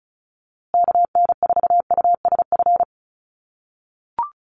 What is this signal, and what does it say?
Morse code, heard as a keyed tone, about 34 wpm, 700 Hz. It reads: KN4USF
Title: USA Call Signs Repeated 2x 35wpm